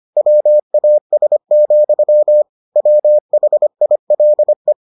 Morse code, heard as a keyed tone, about 25 wpm, 600 Hz. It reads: WAS, WHILE